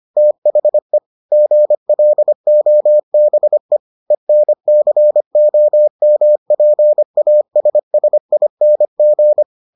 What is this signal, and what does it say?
Morse code, heard as a keyed tone, about 25 wpm, 600 Hz. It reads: THE GLOBE ENCOMPASSING